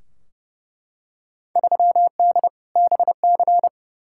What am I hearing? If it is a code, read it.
Morse code, heard as a keyed tone, about 29 wpm, 700 Hz. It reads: 3D BC
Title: Field Day Exchanges Encore - 30wpm